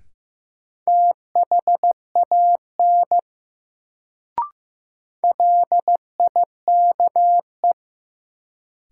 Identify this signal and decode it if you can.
Morse code, heard as a keyed tone, about 15 wpm, 700 Hz. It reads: THAN LIKE
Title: Sentences from Top 100 Words - New Word Drill - 15wpm